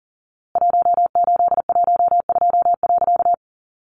Morse code, heard as a keyed tone, about 40 wpm, 700 Hz. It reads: 1812.
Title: A Sound Of Thunder 40wpm